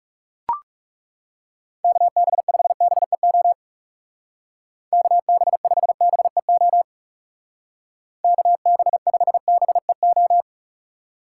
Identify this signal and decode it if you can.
Morse code, heard as a keyed tone, about 41 wpm, 700 Hz. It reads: KB5BEO KB5BEO KB5BEO